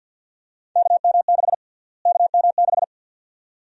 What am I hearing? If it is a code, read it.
Morse code, heard as a keyed tone, about 51 wpm, 700 Hz. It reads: KM6 KM6
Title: USA Call Sign Prefixes - Repeated 2x - 50wpm